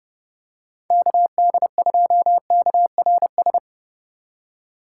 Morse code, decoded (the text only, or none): KD2KRS